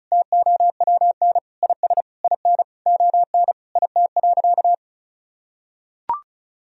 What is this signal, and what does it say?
Morse code, heard as a keyed tone, about 34 wpm, 700 Hz. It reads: TOWN IS IN ON IT.
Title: Sentences from Top 1000 Words - Review Part 4 - 35wpm